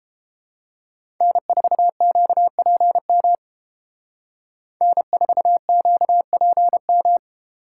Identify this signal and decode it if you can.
Morse code, heard as a keyed tone, about 29 wpm, 700 Hz. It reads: N4QPM N4QPM